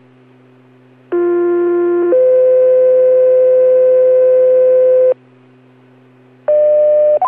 Two-tone page, heard as a one-second tone, then a three-second tone, about 350 then 510 Hz. It is heard from 1.1 to 5.1 s.